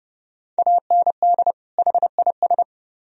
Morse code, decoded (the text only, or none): AND HIS